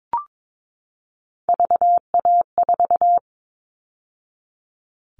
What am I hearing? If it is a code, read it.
Morse code, heard as a keyed tone, about 22 wpm, 700 Hz. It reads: VA4
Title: All Cut Numbers - 3-Digit 22wpm